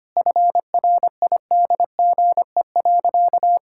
Morse code, heard as a keyed tone, about 25 wpm, 700 Hz. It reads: FRIDGE.
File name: practice / 25wpm / Sentences with 4-Word Phrases - 25wpm